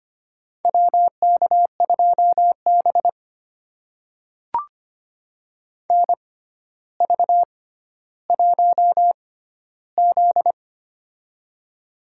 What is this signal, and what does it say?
Morse code, heard as a keyed tone, about 25 wpm, spaced as for 13 wpm, 700 Hz. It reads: WK2B NV1Z
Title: USA Call Signs - 4 Characters - 6x Character Spacing - 25wpm